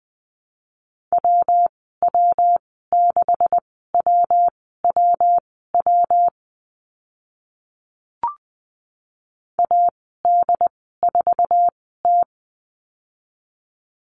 Morse code, heard as a keyed tone, about 20 wpm, spaced as for 14 wpm, 700 Hz. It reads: WW6WWW AD4T
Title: 140 Lesson 23g - Letter f - Call Signs 20wpm